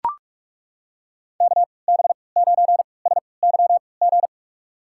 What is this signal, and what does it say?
Morse code, heard as a keyed tone, about 41 wpm, spaced as for 25 wpm, 690 Hz. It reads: KB9SYG
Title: USA Call Signs - 6 Characters - 3x Character Spacing - 45wpm